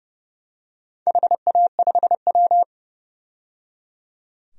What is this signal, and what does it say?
Morse code, heard as a keyed tone, about 31 wpm, 700 Hz. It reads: HA5W
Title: DXCC-100 Call Signs - 4 Characters - 30wpm